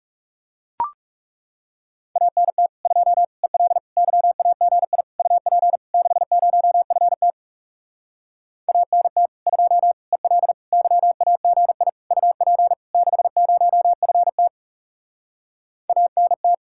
Morse code, heard as a keyed tone, about 37 wpm, 700 Hz. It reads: ANT 2 EL YAGI UP 60FT ANT 2 EL YAGI UP 60FT ANT
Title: Speed-Racing Common QSO Phrases Encore - 30wpm